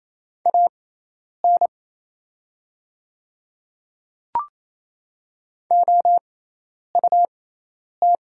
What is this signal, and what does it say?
Morse code, heard as a keyed tone, about 28 wpm, spaced as for 10 wpm, 700 Hz. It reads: AN OUT
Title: 6x Character Spacing - Top 100 Words - 28wpm